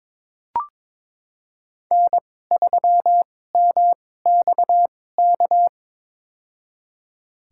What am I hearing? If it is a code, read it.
Morse code, heard as a keyed tone, about 21 wpm, spaced as for 16 wpm, 700 Hz. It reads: N3MXK
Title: USA Call Signs - 2x Character Spacing - 22wpm